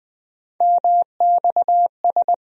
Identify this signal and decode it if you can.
Morse code, heard as a keyed tone, about 20 wpm, 700 Hz. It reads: MXS
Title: USA Call Sign Postfixes - 3 Letters 20wpm